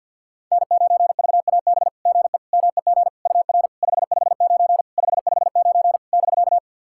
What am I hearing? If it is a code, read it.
Morse code, heard as a keyed tone, about 45 wpm, 690 Hz. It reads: N9VAB GE MEG UR 559 559 <BK>
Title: SOTA QSO Practice 50wpm